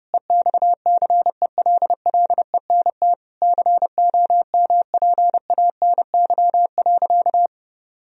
Morse code, heard as a keyed tone, about 32 wpm, 700 Hz. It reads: EXCELLENT COMPANY.
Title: Sentences from Top 1000 Words - Review Part 3 - 30wpm